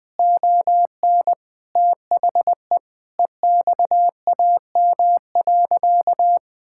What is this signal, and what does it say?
Morse code, heard as a keyed tone, about 20 wpm, 700 Hz. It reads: ON THE EXAM.
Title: Sentences from Top 1000 Words - Review Part 5 - 20wpm